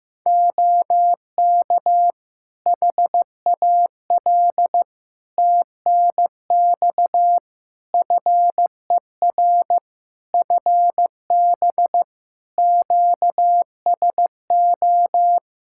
Morse code, heard as a keyed tone, about 16 wpm, 700 Hz. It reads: OK HAL TNX FER FB QSO
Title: Practice QSO - Exchange 4 - Ending - 15wpm